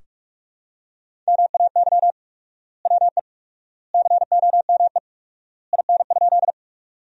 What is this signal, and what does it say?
Morse code, heard as a keyed tone, about 46 wpm, 700 Hz. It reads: MAY WE COME IN?
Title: 4x Word-Spacing - Sentences from Top 100 Words - 45wpm